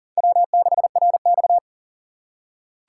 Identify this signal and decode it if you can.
Morse code, heard as a keyed tone, about 40 wpm, 700 Hz. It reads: W6RX
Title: USA Call Signs - 4 Characters - 6x Character Spacing - 40wpm